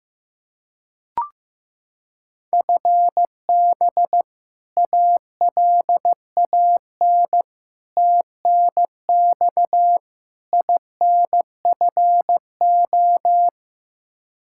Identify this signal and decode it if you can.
Morse code, heard as a keyed tone, about 16 wpm, 700 Hz. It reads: FB ALAN TNX INFO
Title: Practice QSO - Exchange 3 - More Info - 15wpm